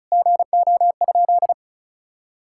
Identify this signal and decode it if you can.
Morse code, heard as a keyed tone, about 34 wpm, 700 Hz. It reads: GO?
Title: Sentences from Top 1600 Words 35wpm